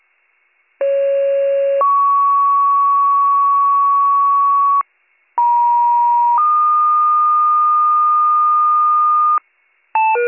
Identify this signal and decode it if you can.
Two-tone page, heard as a one-second tone, then a three-second tone, about 550 then 1120 Hz, about 950 then 1220 Hz. It is heard from 0.8 to 9.4 s.